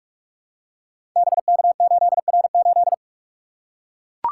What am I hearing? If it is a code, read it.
Morse code, heard as a keyed tone, about 44 wpm, 700 Hz. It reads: BK8R8
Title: Code Groups - 5 Letter:Numbers - 45wpm